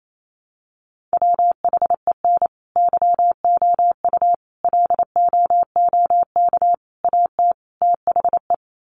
Morse code, heard as a keyed tone, about 28 wpm, 700 Hz. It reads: WHEN YOU LOOK AT THE
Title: Sets of 5 Words 28wpm